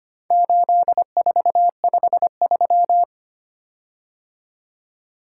Morse code, heard as a keyed tone, about 25 wpm, 690 Hz. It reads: 8453
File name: practice / 25wpm / Four Letter-Number Combinations - 25wpm